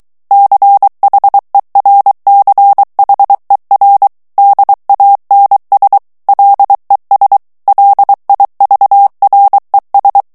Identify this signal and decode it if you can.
Morse code, heard as a keyed tone, about 23 wpm, 800 Hz. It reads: CHERCHER DANS LES LIVRES
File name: morse mp3 / FRANCAIS TEXTE CLAIR 2 VITESSE 22 MOTS MINUTES